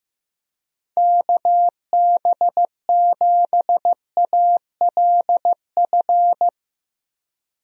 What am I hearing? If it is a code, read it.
Morse code, heard as a keyed tone, about 15 wpm, 700 Hz. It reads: KB7ALF
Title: USA Call Signs - 6 Characters - Repeated 3x - 15wpm